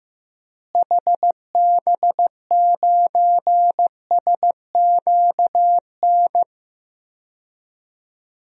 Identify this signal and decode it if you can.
Morse code, heard as a keyed tone, about 15 wpm, 700 Hz. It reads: HB9SQN
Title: DXCC-100 Call Signs - 6 Characters - Repeated 2x - 15wpm